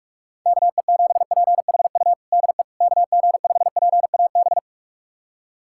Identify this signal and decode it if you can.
Morse code, heard as a keyed tone, about 46 wpm, 700 Hz. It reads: KE7PHU DE KG5PAB
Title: USA Callsign DE Callsign - 6 Characters - 45wpm